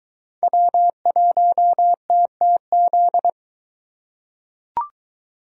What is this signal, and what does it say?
Morse code, heard as a keyed tone, about 22 wpm, 700 Hz. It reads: W1TTZ